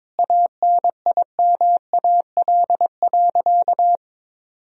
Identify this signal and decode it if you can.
Morse code, heard as a keyed tone, about 22 wpm, 700 Hz. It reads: ANIMAL.
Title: Sentences from Top 1000 Words - Review Part 1 - 22wpm